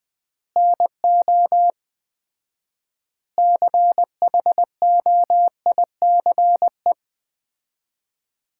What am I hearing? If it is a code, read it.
Morse code, heard as a keyed tone, about 21 wpm, 700 Hz. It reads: NO CHOICE
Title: 4x Word-Spacing - Sets of 5 Words - 20wpm